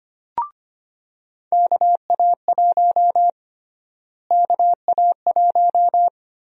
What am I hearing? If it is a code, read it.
Morse code, heard as a keyed tone, about 25 wpm, 700 Hz. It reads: KA1 KA1
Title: USA Call Sign Prefixes - Repeated 2x - 25wpm